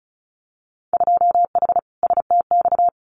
Morse code, heard as a keyed tone, about 36 wpm, 700 Hz. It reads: 2H STX
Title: Winter Field Day Exchanges Encore 35wpm